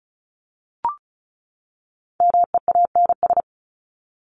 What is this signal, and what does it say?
Morse code, heard as a keyed tone, about 34 wpm, 700 Hz. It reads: MEANS
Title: ES:S Suffix - Top 100 Words with Base Words 35wpm